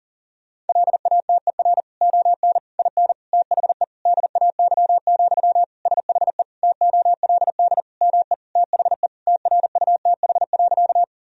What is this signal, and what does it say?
Morse code, heard as a keyed tone, about 40 wpm, 700 Hz. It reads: LATER ON IN THE DAY, SHE TOLD ME THE TRUTH.
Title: Sentences from Top 1000 Words - Review Part 4 - 40wpm